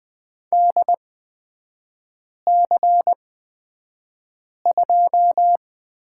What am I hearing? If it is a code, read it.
Morse code, heard as a keyed tone, about 20 wpm, spaced as for 5 wpm, 700 Hz. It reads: DC2
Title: 236 Lesson 38e - Letter q - Three Characters 20wpm